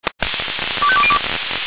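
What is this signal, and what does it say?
Five-tone selective call (ZVEI), heard as quick runs of five tones, tones of about 70 ms. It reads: 25202
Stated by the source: ZVEI five-tone call received by SDR.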